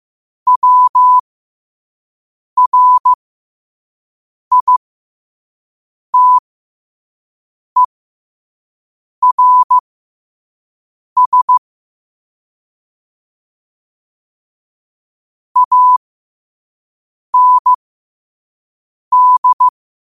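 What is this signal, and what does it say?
Morse code, heard as a keyed tone, about 15 wpm, spaced as for 5 wpm, 1000 Hz. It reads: WRITERS AND
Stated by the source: Morse code practice at 5 wpm.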